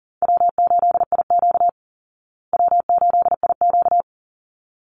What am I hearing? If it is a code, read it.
Morse code, heard as a keyed tone, about 40 wpm, 700 Hz. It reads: W8IQ W8IQ
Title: Call Signs Repeated 3x - 40wpm